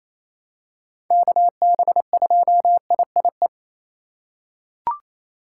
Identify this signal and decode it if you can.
Morse code, heard as a keyed tone, about 28 wpm, 700 Hz. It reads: KB2IIE
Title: USA Call Signs - 6 Characters - 6x Character Spacing - 28wpm